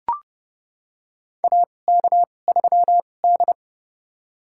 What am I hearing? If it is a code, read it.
Morse code, heard as a keyed tone, about 30 wpm, spaced as for 22 wpm, 700 Hz. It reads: AK3D
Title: USA Call Signs - 4 Characters - 2x Character Spacing - 30wpm